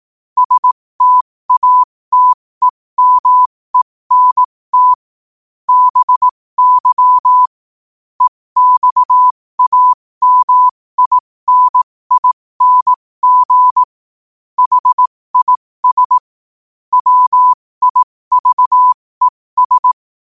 Morse code, heard as a keyed tone, about 18 wpm, spaced as for 15 wpm, 1000 Hz. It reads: STATEMENT BY EXAMINING HIS WIVES
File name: qotdfiles / 15wpm